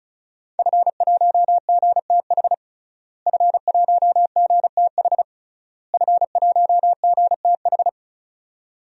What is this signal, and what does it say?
Morse code, heard as a keyed tone, about 36 wpm, 700 Hz. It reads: F1GTH F1GTH F1GTH
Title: DXCC-100 Call Signs - 5 Characters - Repeated 3x - 35wpm